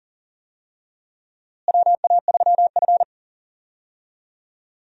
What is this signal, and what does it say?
Morse code, heard as a keyed tone, about 40 wpm, 700 Hz. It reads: WA3F
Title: USA Call Signs - 4 Characters - Repeated 3x - 40wpm